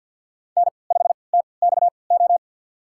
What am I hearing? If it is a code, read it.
Morse code, heard as a keyed tone, about 53 wpm, spaced as for 29 wpm, 700 Hz. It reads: N5TXO
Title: USA Call Signs - 5 Characters - 3x Character Spacing - 50wpm